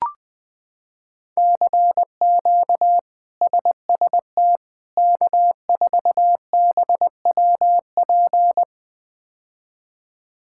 Morse code, heard as a keyed tone, about 21 wpm, 700 Hz. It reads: CQ SST K4BWP
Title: K1USN SST QSO Practice - 20wpm